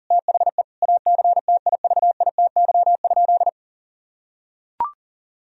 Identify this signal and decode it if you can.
Morse code, heard as a keyed tone, about 40 wpm, 700 Hz. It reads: THE ACTIVITY?
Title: Sentences from Top 1000 Words - Review Part 4 - 40wpm